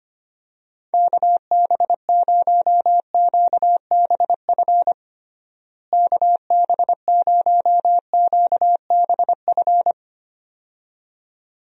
Morse code, heard as a keyed tone, about 25 wpm, 700 Hz. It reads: KB0QBF KB0QBF
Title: USA Call Signs - 6 Characters - Repeated 2x - 25wpm